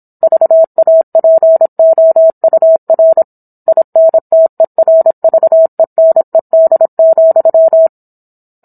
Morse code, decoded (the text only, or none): VAPOUR INTERVENED,